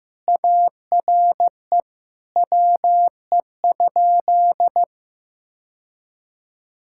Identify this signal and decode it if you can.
Morse code, heard as a keyed tone, about 14 wpm, 700 Hz. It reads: ARE WE?
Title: Sentences from Top 1600 Words 15wpm